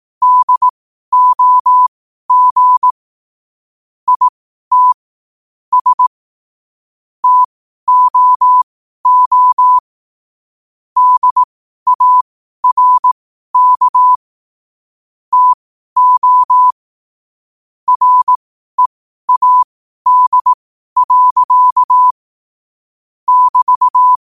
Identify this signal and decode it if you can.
Morse code, heard as a keyed tone, about 18 wpm, spaced as for 12 wpm, 1000 Hz. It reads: DOG IT S TOO DARK TO READ. <BT>
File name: qotdfiles / 12wpm